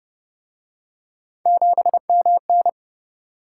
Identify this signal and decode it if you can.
Morse code, heard as a keyed tone, about 31 wpm, 700 Hz. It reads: 7MN